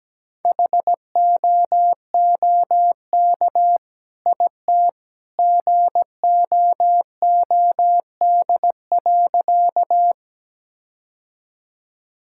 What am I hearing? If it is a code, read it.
Morse code, heard as a keyed tone, about 17 wpm, 700 Hz. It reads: HOOK IT GOOD.
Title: Sentences from Top 1600 Words 17wpm